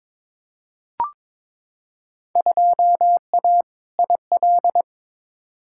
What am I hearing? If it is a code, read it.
Morse code, heard as a keyed tone, about 22 wpm, 700 Hz. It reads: 2A IL
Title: Field Day Exchanges - 22wpm